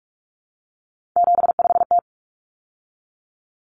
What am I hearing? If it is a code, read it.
Morse code, heard as a keyed tone, about 44 wpm, 700 Hz. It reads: 75T
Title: Common Cut Numbers - 3-Digit 45wpm